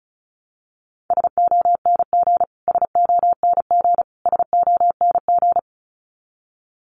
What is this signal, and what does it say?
Morse code, heard as a keyed tone, about 34 wpm, 700 Hz. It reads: SONG SONG SONG
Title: Top 500 Words Repeated 35wpm